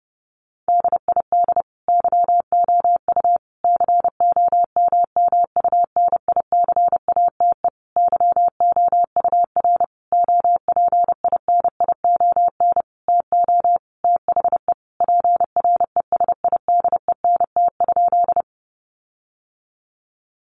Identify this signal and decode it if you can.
Morse code, heard as a keyed tone, about 30 wpm, 700 Hz. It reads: DID YOU COMMUNICATE YOUR OPINION TO THE PRESIDENT?
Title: Sentences from Top 1300 Words 30wpm